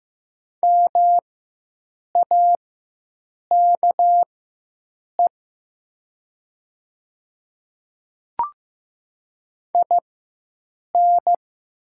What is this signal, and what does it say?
Morse code, heard as a keyed tone, about 15 wpm, spaced as for 7 wpm, 700 Hz. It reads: MAKE IN
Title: 4x Character Spacing - Top 100 Words - 15wpm